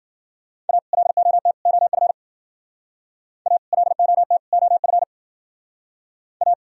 Morse code, heard as a keyed tone, about 52 wpm, 700 Hz. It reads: A LOT OF A LOT OF A